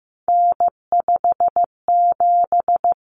Morse code, heard as a keyed tone, about 15 wpm, 700 Hz. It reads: N57
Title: Common Cut Numbers - 3-Digit 15wpm